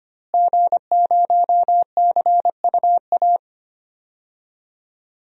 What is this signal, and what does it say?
Morse code, heard as a keyed tone, about 25 wpm, 700 Hz. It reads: G0CUA
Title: DXCC-100 Call Signs - 5 Characters - Repeated 3x - 25wpm